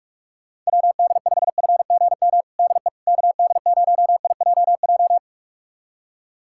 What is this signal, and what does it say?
Morse code, heard as a keyed tone, about 46 wpm, 700 Hz. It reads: WD5FGM DE KD0IJJ
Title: USA Callsign DE Callsign - 6 Characters - 45wpm